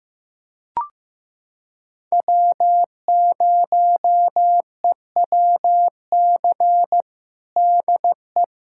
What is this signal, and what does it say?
Morse code, heard as a keyed tone, about 16 wpm, 700 Hz. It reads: W0EWC DE
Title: USA Callsign DE Callsign 15wpm